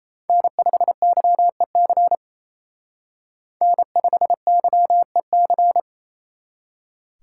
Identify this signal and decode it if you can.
Morse code, heard as a keyed tone, about 35 wpm, 700 Hz. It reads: N5YEC N5YEC
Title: Speed Racing - USA 1x3 Call Signs 28wpm